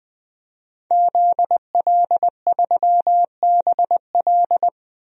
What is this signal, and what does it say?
Morse code, heard as a keyed tone, about 20 wpm, 700 Hz. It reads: ZL3BL
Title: DXCC-100 Call Signs - 5 Characters - 20wpm